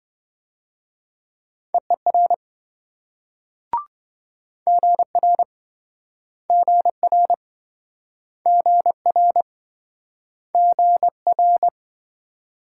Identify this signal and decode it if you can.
Morse code, heard as a keyed tone, about 28 wpm, 700 Hz. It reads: EER GR GR GR GR